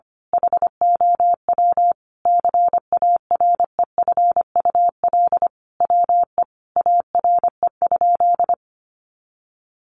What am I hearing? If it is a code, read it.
Morse code, heard as a keyed tone, about 25 wpm, 700 Hz. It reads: HOW CAREFUL WE ARE?
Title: A Sound Of Thunder 25wpm